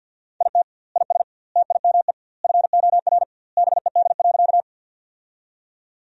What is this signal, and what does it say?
Morse code, heard as a keyed tone, about 46 wpm, 700 Hz. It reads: IT IS TIME FOR BED.
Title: 2x Word-Spacing - Sentences from Top 500 Words - 50wpm